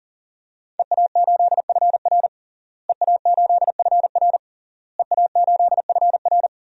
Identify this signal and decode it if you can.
Morse code, heard as a keyed tone, about 40 wpm, 700 Hz. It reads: EA8FR EA8FR EA8FR
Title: DXCC-100 Call Signs - 5 Characters - Repeated 3x - 40wpm